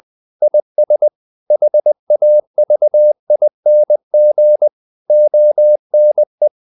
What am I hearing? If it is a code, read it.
Morse code, heard as a keyed tone, about 20 wpm, 600 Hz. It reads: IS HAVING ONE
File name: CW / Practice / EnglishQuotations0320